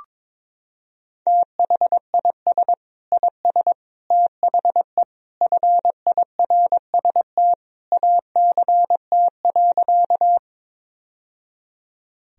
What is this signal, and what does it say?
Morse code, heard as a keyed tone, about 22 wpm, 700 Hz. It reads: THIS IS THE FIRST ACT.
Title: Sentences from Top 1000 Words - Review Part 1 - 22wpm